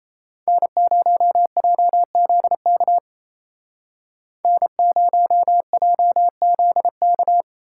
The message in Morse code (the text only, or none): N0JZK N0JZK